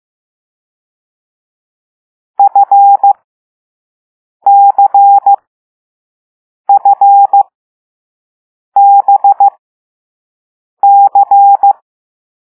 Morse code, heard as a keyed tone, about 15 wpm, spaced as for 5 wpm, 800 Hz. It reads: FCFBC
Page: Play Lesson 6 C - all letters in previous lessons